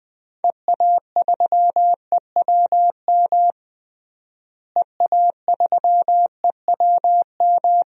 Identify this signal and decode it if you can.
Morse code, heard as a keyed tone, about 20 wpm, 700 Hz. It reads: EA3EWM EA3EWM
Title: DXCC-100 Call Signs - 6 Characters - Repeated 3x - 20wpm